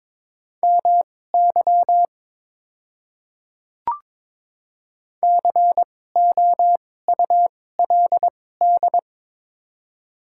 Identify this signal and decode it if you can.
Morse code, heard as a keyed tone, about 22 wpm, spaced as for 16 wpm, 700 Hz. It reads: MY COULD
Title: 2x Character Spacing - Top 100 Words - 22wpm